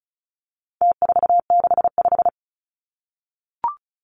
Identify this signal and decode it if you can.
Morse code, heard as a keyed tone, about 34 wpm, 700 Hz. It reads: T465
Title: Numbers - Contest Style - 4-Digit 35wpm